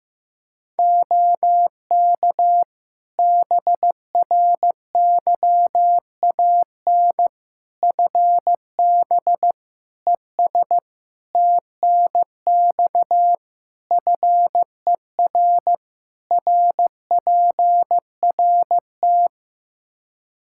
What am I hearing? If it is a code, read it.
Morse code, heard as a keyed tone, about 16 wpm, 700 Hz. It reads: OK BRYAN FB ES TNX FER RPRT
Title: Practice QSO - Exchange 2 - Info - 15wpm